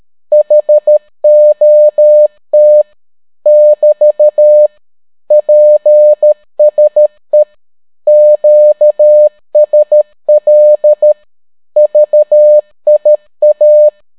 Morse code, heard as a keyed tone, about 13 wpm, 600 Hz. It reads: HOT <BT> PSE QSL VIA